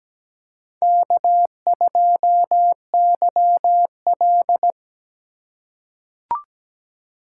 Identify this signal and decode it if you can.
Morse code, heard as a keyed tone, about 17 wpm, 700 Hz. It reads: K2YL
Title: USA Call Signs - 4 Characters - 4x Character Spacing - 17wpm